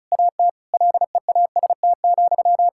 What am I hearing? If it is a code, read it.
Morse code, heard as a keyed tone, about 36 wpm, 700 Hz. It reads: AT LEAST,
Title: Sentences from Top 1000 Words - Review Part 6 - 35wpm